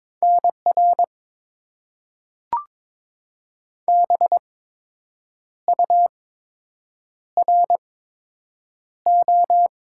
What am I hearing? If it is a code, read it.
Morse code, heard as a keyed tone, about 22 wpm, spaced as for 7 wpm, 700 Hz. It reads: NR BURO
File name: Top 90 QSO Elements - 8x Character Spacing - 22wpm